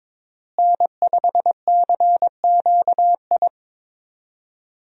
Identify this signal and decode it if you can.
Morse code, heard as a keyed tone, about 21 wpm, 700 Hz. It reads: N5CQI
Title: USA Call Signs - 5 Characters - Repeated 3x - 22wpm